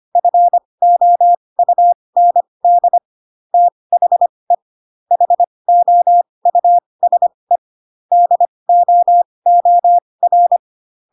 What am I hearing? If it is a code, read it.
Morse code, heard as a keyed tone, about 25 wpm, spaced as for 20 wpm, 700 Hz. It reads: FOUND THE HOUSE DOOR
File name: Pinocchio / Pinocchio ch4 15 25 ik1hge